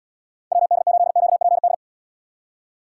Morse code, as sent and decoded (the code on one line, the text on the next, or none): .-- -.. ---.. --.- .--. -..
WD8QPD